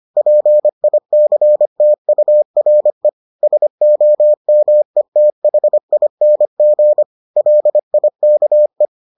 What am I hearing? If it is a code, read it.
Morse code, heard as a keyed tone, about 25 wpm, 600 Hz. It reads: PICTURE SOMETHING LIKE